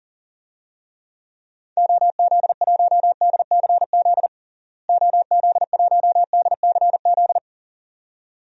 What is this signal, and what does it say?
Morse code, heard as a keyed tone, about 40 wpm, 700 Hz. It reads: OZ1DCZ OZ1DCZ
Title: DXCC-100 Call Signs - 6 Characters - Repeated 2x - 40wpm